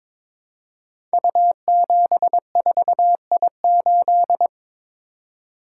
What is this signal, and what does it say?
Morse code, heard as a keyed tone, about 22 wpm, 700 Hz. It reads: U74I8
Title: Code Groups - 5 Letter:Numbers - 22wpm